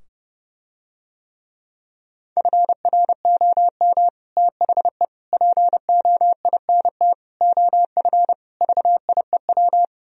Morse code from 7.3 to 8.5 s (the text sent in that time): OF